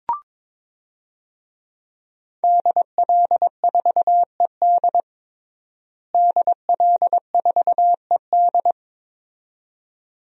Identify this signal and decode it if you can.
Morse code, heard as a keyed tone, about 21 wpm, 700 Hz. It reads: DL4ED DL4ED
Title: DXCC-100 Call Signs - 5 Characters - Repeated 2x - 22wpm